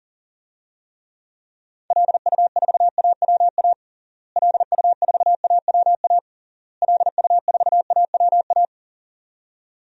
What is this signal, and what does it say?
Morse code, heard as a keyed tone, about 40 wpm, 700 Hz. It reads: LU4AWA LU4AWA LU4AWA
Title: DXCC-100 Call Signs - 6 Characters - Repeated 3x - 40wpm